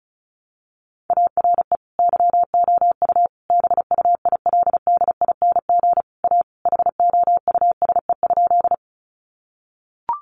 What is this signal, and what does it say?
Morse code, heard as a keyed tone, about 34 wpm, 700 Hz. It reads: ARE YOU BUILDING A HOUSE?
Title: Sentences from Top 700 Words 35wpm